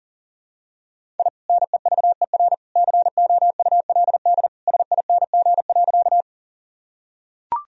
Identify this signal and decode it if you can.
Morse code, heard as a keyed tone, about 40 wpm, 700 Hz. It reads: I NEVER COULD SING.
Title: Sentences from Top 1000 Words - Review Part 2 - 40wpm